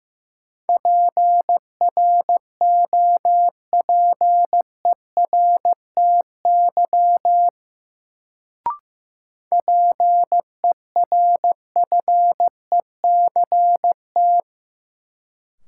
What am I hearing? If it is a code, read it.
Morse code, heard as a keyed tone, about 15 wpm, 700 Hz. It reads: PROPERTY PERFECT
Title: Sentences from Top 1000 Words - New Word Drill - 15wpm